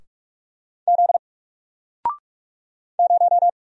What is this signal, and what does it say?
Morse code, heard as a keyed tone, about 44 wpm, 700 Hz. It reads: Z 0
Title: Single Letter-Number 45wpm